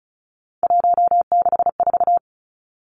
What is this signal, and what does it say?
Morse code, heard as a keyed tone, about 34 wpm, 700 Hz. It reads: 164
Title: Numbers - 3-Digit 35wpm